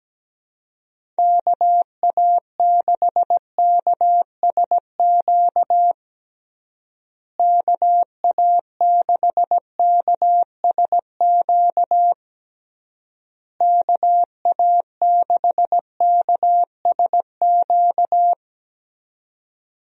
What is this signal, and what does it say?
Morse code, heard as a keyed tone, about 17 wpm, 700 Hz. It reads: KA6KSQ KA6KSQ KA6KSQ
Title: USA Call Signs - 6 Characters - Repeated 3x - 17wpm